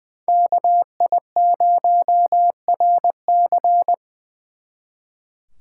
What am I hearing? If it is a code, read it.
Morse code, heard as a keyed tone, about 24 wpm, 700 Hz. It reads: KI0RC
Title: Speed Racing - USA 2x2 Call Signs 20wpm